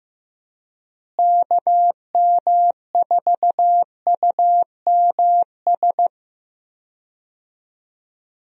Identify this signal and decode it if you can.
Morse code, heard as a keyed tone, about 15 wpm, 700 Hz. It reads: KM4UMS
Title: USA Call Signs - 6 Characters - Repeated 2x - 15wpm